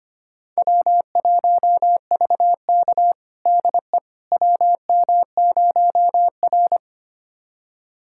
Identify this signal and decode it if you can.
Morse code, heard as a keyed tone, about 25 wpm, 700 Hz. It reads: W1VK DE WM0R
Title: USA Callsign DE Callsign - 4 Characters - 25wpm